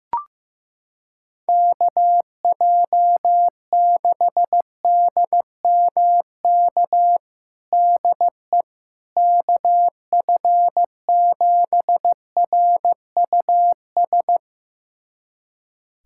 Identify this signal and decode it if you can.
Morse code, heard as a keyed tone, about 15 wpm, 700 Hz. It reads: KJ6DMK DE KF7RUS
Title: USA Callsign DE Callsign - 6 Characters - 15wpm